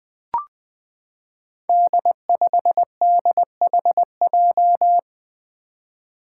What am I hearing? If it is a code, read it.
Morse code, heard as a keyed tone, about 20 wpm, 700 Hz. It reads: D5DHJ
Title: Five Challenging Letter-Number Combinations 20wpm